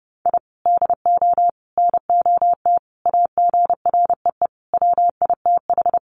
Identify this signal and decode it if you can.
Morse code, heard as a keyed tone, about 30 wpm, 700 Hz. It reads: I DO NOT AGREE WITH
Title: Sets of 5 Words Encore 30wpm